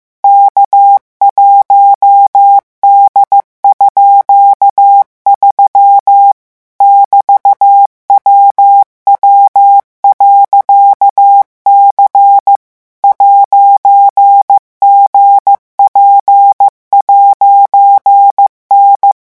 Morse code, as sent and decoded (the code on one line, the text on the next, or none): -.- .---- -.. ..--.- ...-- / -...- .-- .-- .-.-.- -.-. / .----. --. .--. .----. -.
K1D_3 <BT>WW.C 'GP'N